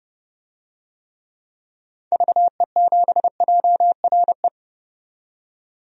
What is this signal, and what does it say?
Morse code, heard as a keyed tone, about 30 wpm, 700 Hz. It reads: VE7JRE
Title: DXCC-100 Call Signs - 6 Characters - 30wpm